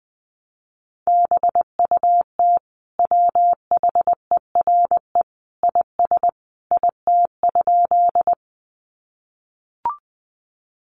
Morse code, heard as a keyed tone, about 20 wpm, 700 Hz. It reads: BUT WHERE IS IT?
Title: Sentences from Top 200 Words 20wpm